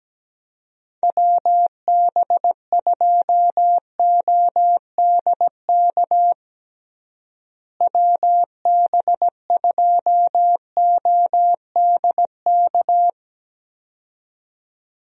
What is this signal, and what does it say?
Morse code, heard as a keyed tone, about 17 wpm, 700 Hz. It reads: WB2ODK WB2ODK
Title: USA Call Signs - 6 Characters - Repeated 2x - 17wpm